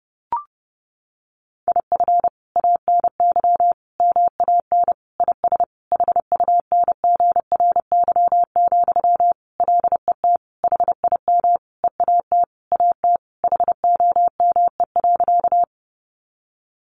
Morse code, heard as a keyed tone, about 30 wpm, 700 Hz. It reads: IF ANY MAN IS HUNGRY, LET HIM EAT AT HOME.
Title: Sentences from Top 1400 Words 30wpm